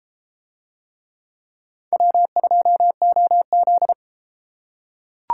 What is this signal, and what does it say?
Morse code, heard as a keyed tone, about 32 wpm, 700 Hz. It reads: W2OZ